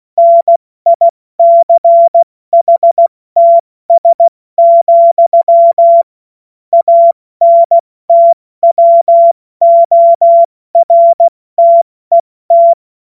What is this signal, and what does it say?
Morse code, heard as a keyed tone, about 16 wpm, spaced as for 14 wpm, 680 Hz. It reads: NICHTS, ANTWORTET
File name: Morsekurs / Witze / Witz 31. Mai 2022 70 F80